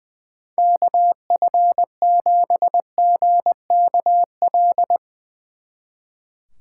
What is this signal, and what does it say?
Morse code, heard as a keyed tone, about 22 wpm, 700 Hz. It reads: KF7GKL
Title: Speed Racing - USA 2x3 Call Signs 20wpm